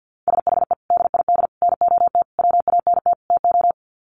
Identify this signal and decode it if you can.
Morse code, heard as a keyed tone, about 48 wpm, 700 Hz. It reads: SHE DID NOT WANT TO
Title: Sets of 5 Words 50wpm